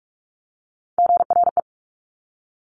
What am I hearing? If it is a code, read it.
Morse code, heard as a keyed tone, about 46 wpm, 700 Hz. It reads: GRE
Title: Top Three Letter Combinations in Words 45wpm